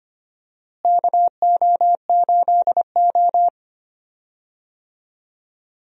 Morse code, heard as a keyed tone, about 25 wpm, 700 Hz. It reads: KO8O
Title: USA Call Signs - 4 Characters - Repeated 2x - 25wpm